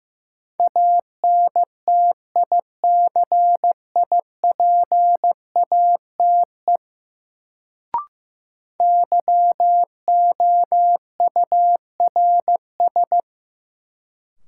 Morse code, heard as a keyed tone, about 15 wpm, 700 Hz. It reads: ANTICIPATE YOURS
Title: Sentences from Top 1500 Words - New Word Drill - 15wpm